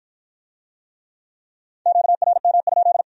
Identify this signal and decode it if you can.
Morse code, heard as a keyed tone, about 47 wpm, 700 Hz. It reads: QRM?